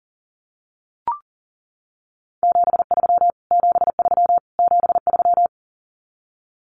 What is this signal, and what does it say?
Morse code, heard as a keyed tone, about 40 wpm, 700 Hz. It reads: 73 73 73
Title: Abbreviations Repeated 3x - 40wpm